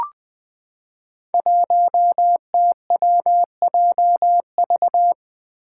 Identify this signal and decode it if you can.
Morse code, heard as a keyed tone, about 20 wpm, 700 Hz. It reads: 1TWJV
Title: Five Challenging Letter-Number Combinations 20wpm